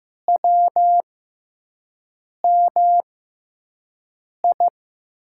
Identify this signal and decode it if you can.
Morse code, heard as a keyed tone, about 15 wpm, spaced as for 5 wpm, 700 Hz. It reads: WMI
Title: USA Call Signs - 5 Characters - 6x Character Spacing - 15wpm